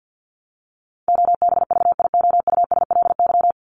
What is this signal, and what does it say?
Morse code, heard as a keyed tone, about 50 wpm, 700 Hz. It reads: OBVIOUSLY
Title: Top 1501-1750 Words 50wpm